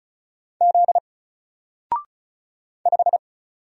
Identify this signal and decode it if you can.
Morse code, heard as a keyed tone, about 34 wpm, 700 Hz. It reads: Z 5
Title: Single Character (Letter:Number:Punctuation:Prosign) 35wpm